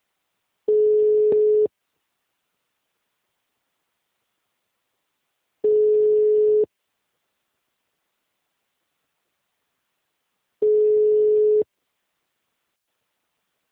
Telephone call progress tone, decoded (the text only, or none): ringback tone